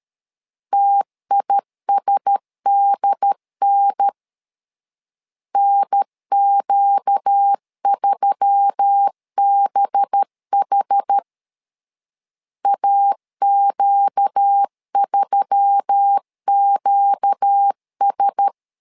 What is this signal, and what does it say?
Morse code, heard as a keyed tone, about 13 wpm, 800 Hz. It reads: TISDN NQ3BH AQ3QS
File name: LEZIONI / 5 LBVC / 5 LBV 12wpm 1855